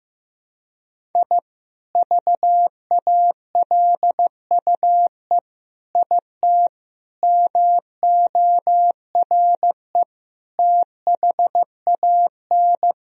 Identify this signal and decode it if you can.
Morse code, heard as a keyed tone, about 15 wpm, 700 Hz. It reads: I VALUE IT MORE THAN
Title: Sentences from Top 1000 Words - Review Part 3 - 15wpm